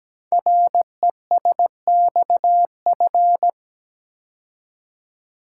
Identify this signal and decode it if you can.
Morse code, heard as a keyed tone, about 17 wpm, 700 Hz. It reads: RESXF
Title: Code Groups - 5 Letters - 17wpm